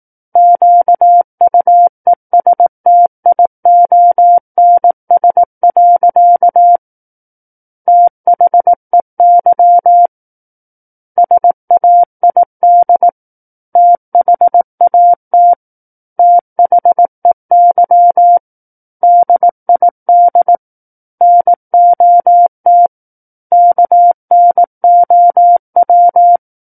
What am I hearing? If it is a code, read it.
Morse code, heard as a keyed tone, about 18 wpm, 700 Hz. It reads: QUESTIONS. THEY SAID THAT THEY DID NOT KNOW
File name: Book One / 09-Chapter 9 - 18 WPM